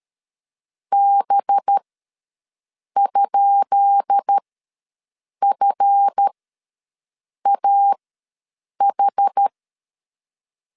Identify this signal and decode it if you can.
Morse code, heard as a keyed tone, about 13 wpm, spaced as for 6 wpm, 800 Hz. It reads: B?FAH
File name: LEZIONI / 10 SIGNS / 10B INT ws 12wpm 3527